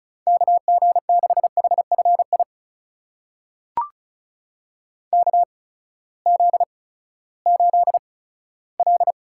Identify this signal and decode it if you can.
Morse code, heard as a keyed tone, about 36 wpm, spaced as for 18 wpm, 700 Hz. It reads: KG6HFI KZ8L
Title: USA Call Signs - 8x Character Spacing - 35wpm